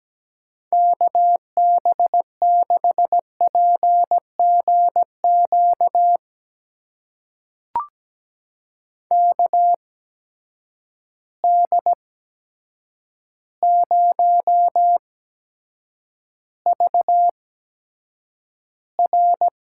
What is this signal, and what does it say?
Morse code, heard as a keyed tone, about 17 wpm, spaced as for 8 wpm, 700 Hz. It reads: KB6PGQ KD0VR
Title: USA Call Signs - 6 Characters - 8x Character Spacing - 17wpm